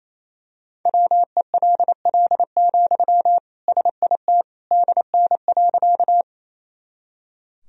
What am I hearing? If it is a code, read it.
Morse code, heard as a keyed tone, about 28 wpm, 700 Hz. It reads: WELL, SIT DN.
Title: Sentences from Top 500 Words and Abbreviations 28wpm